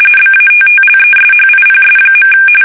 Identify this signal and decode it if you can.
RTTY, about 45.45 baud. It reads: BATTLESHIPS.
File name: fsk850 / battleshipnj